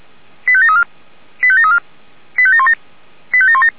Five-tone selective call (ZVEI), heard as quick runs of five tones, tones of about 70 ms. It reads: 85734 85734 85717 85717